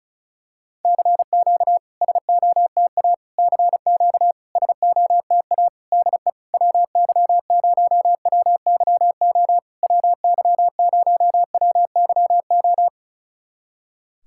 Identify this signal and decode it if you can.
Morse code, heard as a keyed tone, about 36 wpm, 700 Hz. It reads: CQ SOTA CQ SOTA DE WY0WYO WY0WYO
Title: SOTA QSO Practice 35wpm